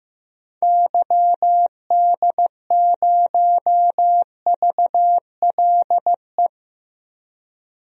Morse code, heard as a keyed tone, about 15 wpm, 700 Hz. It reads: YD0VLE
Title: DXCC-100 Call Signs - 6 Characters - Repeated 3x - 15wpm